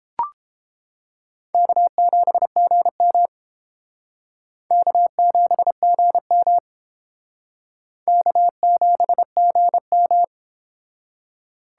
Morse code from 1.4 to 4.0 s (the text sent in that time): K7GM